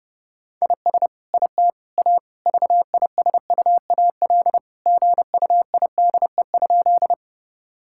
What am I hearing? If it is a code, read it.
Morse code, heard as a keyed tone, about 30 wpm, 700 Hz. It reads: IS IT A VISUAL GUIDE?
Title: Sentences from Top 1000 Words - Review Part 6 - 30wpm